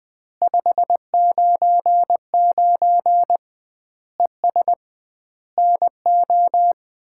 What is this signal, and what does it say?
Morse code, heard as a keyed tone, about 20 wpm, 700 Hz. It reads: 599 ES NO
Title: Common QSO Phrases - 2x Word-Spacing - 20wpm